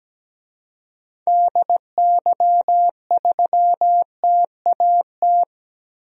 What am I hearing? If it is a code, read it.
Morse code, heard as a keyed tone, about 17 wpm, 700 Hz. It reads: DY3TAT
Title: DXCC-100 Call Signs - 6 Characters - 17wpm